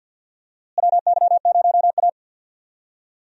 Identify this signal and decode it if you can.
Morse code, heard as a keyed tone, about 52 wpm, 700 Hz. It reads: WY0A
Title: USA Call Signs - 4 Characters - Repeated 2x - 50wpm